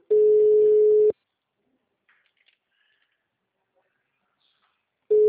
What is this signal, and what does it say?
Ringback tone (Europe, 425 Hz).